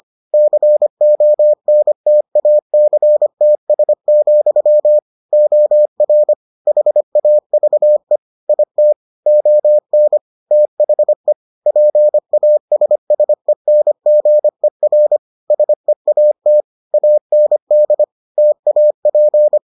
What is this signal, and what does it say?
Morse code, heard as a keyed tone, about 25 wpm, 600 Hz. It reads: CONTACTS, OR HAVE IT ON THE PASSENGER SEAT AND TAP